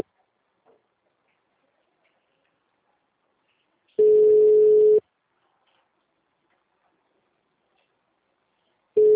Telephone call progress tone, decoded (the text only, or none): ringback tone